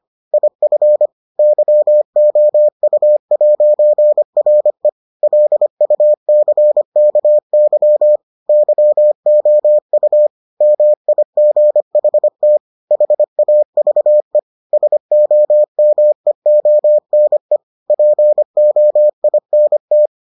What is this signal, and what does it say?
Morse code, heard as a keyed tone, about 25 wpm, 600 Hz. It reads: IF YOU'RE LUCKY YOU MIGHT HAVE SOMEONE POINT